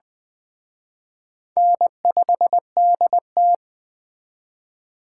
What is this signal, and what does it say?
Morse code, heard as a keyed tone, about 19 wpm, 700 Hz. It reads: N5DT